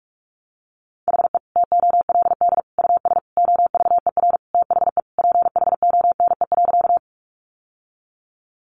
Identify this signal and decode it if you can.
Morse code, heard as a keyed tone, about 44 wpm, 700 Hz. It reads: HE TOLD US OVER THE PHONE.
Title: Sentences from Top 1100 Words 45wpm